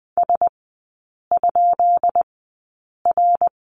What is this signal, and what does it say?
Morse code, heard as a keyed tone, about 20 wpm, spaced as for 8 wpm, 700 Hz. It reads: S?R
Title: 152 Lesson 25e - Letter p - Three Characters 20wpm